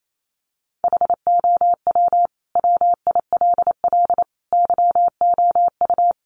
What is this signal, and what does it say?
Morse code, heard as a keyed tone, about 28 wpm, 690 Hz. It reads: HOW WILL YOU
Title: Sentences from Top 900 Words 28wpm